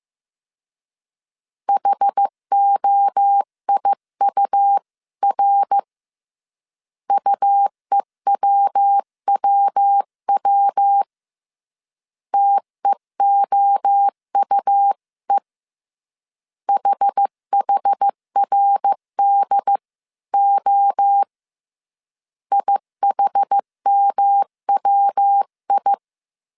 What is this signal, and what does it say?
Morse code, heard as a keyed tone, about 15 wpm, 800 Hz. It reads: HOIUR UEWWW TEOUE HHRDO IHMWI
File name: CW / DET / 1 3 1500C S C 14wpm 2458